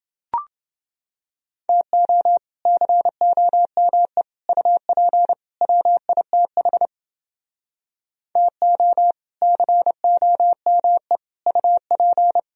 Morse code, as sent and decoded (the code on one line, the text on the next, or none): - --- / -.-. --- -- . / ..- .--. / .-- .. - .... / - --- / -.-. --- -- . / ..- .--.
TO COME UP WITH TO COME UP